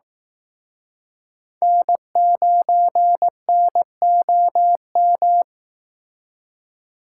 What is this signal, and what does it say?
Morse code, heard as a keyed tone, about 18 wpm, 700 Hz. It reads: N9NOM
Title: Speed Racing - USA 1x3 Call Signs 15wpm